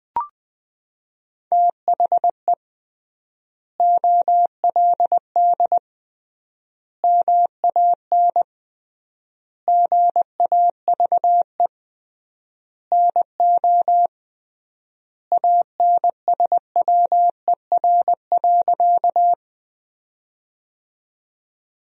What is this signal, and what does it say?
Morse code, heard as a keyed tone, about 20 wpm, 700 Hz. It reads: THE OLD MAN GAVE NO ANSWER.
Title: 3x Word-Spacing - Sentences from Top 500 Words - 20wpm